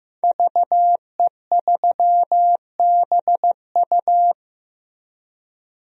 Morse code, heard as a keyed tone, about 15 wpm, 700 Hz. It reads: VE3BU
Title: DXCC-100 Call Signs - 5 Characters - Repeated 3x - 15wpm